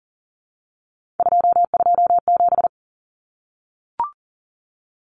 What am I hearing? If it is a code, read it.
Morse code, heard as a keyed tone, about 40 wpm, 700 Hz. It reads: 227
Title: Common Cut Numbers - 3-Digit 40wpm